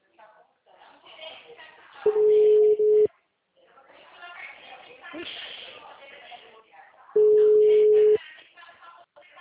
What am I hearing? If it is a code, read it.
Ringback tone (Europe, 425 Hz), with speech.